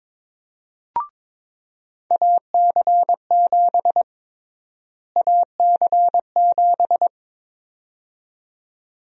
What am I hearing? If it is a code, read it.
Morse code, heard as a keyed tone, about 22 wpm, 700 Hz. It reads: AC7 AC7
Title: USA Call Sign Prefixes - Repeated 2x - 22wpm